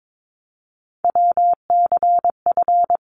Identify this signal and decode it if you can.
Morse code, heard as a keyed tone, about 22 wpm, 700 Hz. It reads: WCF
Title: ARRL Sections 22wpm